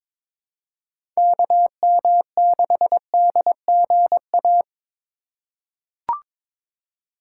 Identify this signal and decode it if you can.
Morse code, heard as a keyed tone, about 21 wpm, 700 Hz. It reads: KM6DGA
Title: USA Call Signs - 6 Characters - 2x Character Spacing - 22wpm